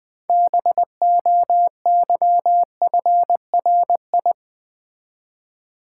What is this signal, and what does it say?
Morse code, heard as a keyed tone, about 20 wpm, 700 Hz. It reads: BOYFRI
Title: Build A Word - 3 Letters At A Time - 9-Letter Words - 20wpm